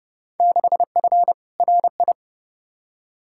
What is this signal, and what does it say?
Morse code, heard as a keyed tone, about 30 wpm, 700 Hz. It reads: 6F RI
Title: Field Day Exchanges Encore - 30wpm